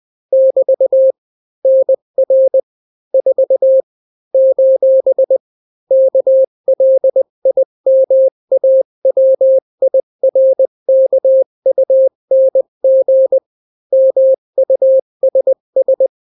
Morse code, heard as a keyed tone, about 20 wpm, 530 Hz. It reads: <BT> NR 4 : KLIMAWIRKUNG MUSS
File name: daily / 18wpm